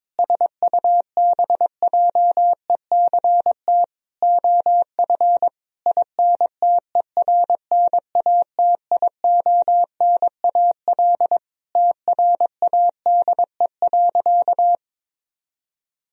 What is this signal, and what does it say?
Morse code, heard as a keyed tone, about 21 wpm, 700 Hz. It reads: SUBJECT OF INTERNATIONAL TRADE.
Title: Sentences from Top 1000 Words - Review Part 4 - 22wpm